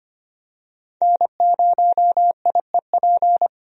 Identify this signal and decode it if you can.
Morse code, heard as a keyed tone, about 25 wpm, 700 Hz. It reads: N0IEP
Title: USA Call Signs - 5 Characters - Repeated 3x - 25wpm